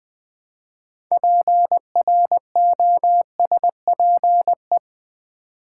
Morse code, heard as a keyed tone, about 21 wpm, 700 Hz. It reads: PROSPE
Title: Build A Word - 3 Letters At A Time - 8-Letter Words - 20wpm